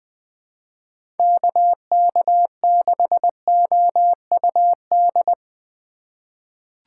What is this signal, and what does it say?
Morse code, heard as a keyed tone, about 20 wpm, 700 Hz. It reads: KK6OUD
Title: USA Call Signs - 6 Characters - 4x Character Spacing - 20wpm